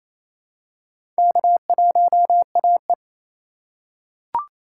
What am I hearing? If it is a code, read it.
Morse code, heard as a keyed tone, about 28 wpm, 700 Hz. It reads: K1AE
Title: USA Call Signs - 4 Characters - 2x Character Spacing - 28wpm